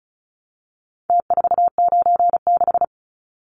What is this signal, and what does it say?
Morse code, heard as a keyed tone, about 34 wpm, 700 Hz. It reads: T496
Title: Numbers - Contest Style - 4-Digit 35wpm